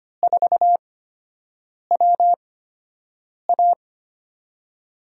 Morse code, heard as a keyed tone, about 25 wpm, spaced as for 7 wpm, 700 Hz. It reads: 4WA
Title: USA Call Signs - 5 Characters - 8x Character Spacing - 25wpm